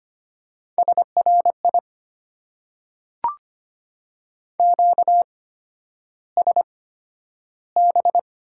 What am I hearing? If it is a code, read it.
Morse code, heard as a keyed tone, about 25 wpm, spaced as for 11 wpm, 690 Hz. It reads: SRI QSB
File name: Top 90 QSO Elements - 8x Character Spacing - 25wpm